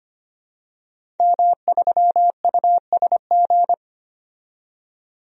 Morse code, heard as a keyed tone, about 25 wpm, 700 Hz. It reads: M3USG
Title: DXCC-100 Call Signs - 5 Characters - 25wpm